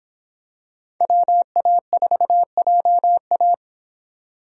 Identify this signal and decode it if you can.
Morse code, heard as a keyed tone, about 25 wpm, 700 Hz. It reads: WA4JA